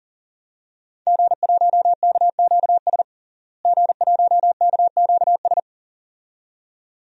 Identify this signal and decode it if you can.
Morse code, heard as a keyed tone, about 37 wpm, 700 Hz. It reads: G1KQS G1KQS
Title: DXCC-100 Call Signs - 5 Characters - Repeated 2x - 40wpm